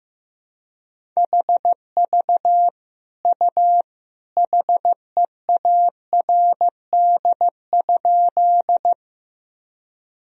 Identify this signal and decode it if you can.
Morse code, heard as a keyed tone, about 14 wpm, 700 Hz. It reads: HV U HEARD?
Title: Sentences from Top 400 Words and Abbreviations 15wpm